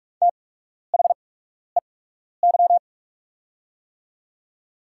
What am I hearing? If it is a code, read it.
Morse code, heard as a keyed tone, about 45 wpm, spaced as for 12 wpm, 690 Hz. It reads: THEY
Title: 8x Character Spacing - Top 300 Words - 45wpm